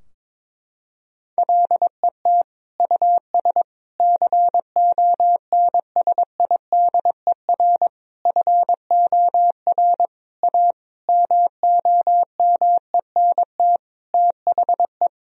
Morse code, heard as a keyed tone, about 22 wpm, 700 Hz. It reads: LET US CONSIDER FOR A MOMENT THE
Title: Sentences from Top 1000 Words - Review Part 4 - 22wpm